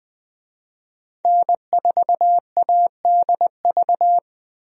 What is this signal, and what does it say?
Morse code, heard as a keyed tone, about 20 wpm, 700 Hz. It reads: N4ADV
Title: USA Call Signs - 5 Characters - Repeated 3x - 20wpm